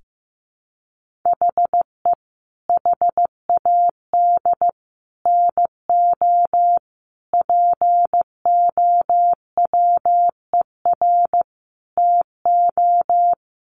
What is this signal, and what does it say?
Morse code, heard as a keyed tone, about 14 wpm, 700 Hz. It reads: HE HAD NO POWER TO
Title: Sentences from Top 1400 Words 15wpm